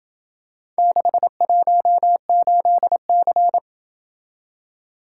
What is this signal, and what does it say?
Morse code, heard as a keyed tone, about 27 wpm, 700 Hz. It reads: 618C